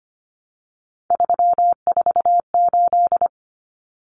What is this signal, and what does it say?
Morse code, heard as a keyed tone, about 25 wpm, 700 Hz. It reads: 348
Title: Common Cut Numbers - 3-Digit 25wpm